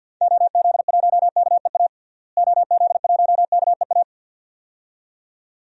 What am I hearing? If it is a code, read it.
Morse code, heard as a keyed tone, about 53 wpm, 700 Hz. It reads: OZ1KEA OZ1KEA
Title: DXCC-100 Call Signs - 6 Characters - Repeated 2x - 50wpm